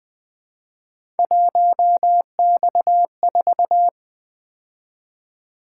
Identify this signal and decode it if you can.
Morse code, heard as a keyed tone, about 20 wpm, 700 Hz. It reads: 1X4
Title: Three Letter-Number Combinations - 20wpm